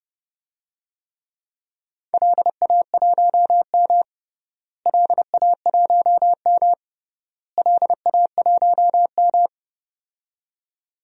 Morse code, heard as a keyed tone, about 30 wpm, 700 Hz. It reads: LA1M LA1M LA1M
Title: DXCC-100 Call Signs - 4 Characters - Repeated 3x - 30wpm